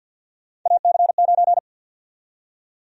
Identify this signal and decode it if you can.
Morse code, heard as a keyed tone, about 50 wpm, 700 Hz. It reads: AC9
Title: USA Call Sign Prefixes - 50wpm